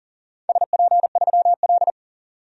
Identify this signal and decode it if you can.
Morse code, heard as a keyed tone, about 41 wpm, 700 Hz. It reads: SP3L
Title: DXCC-100 Call Signs - 4 Characters - 40wpm